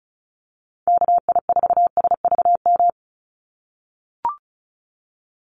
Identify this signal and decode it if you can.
Morse code, heard as a keyed tone, about 34 wpm, 700 Hz. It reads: KI4SVM
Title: NA SOTA Chaser Callsigns 35wpm